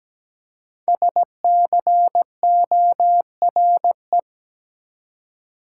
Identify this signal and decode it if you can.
Morse code, heard as a keyed tone, about 17 wpm, 700 Hz. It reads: SCORE
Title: Five Letter Words - 17wpm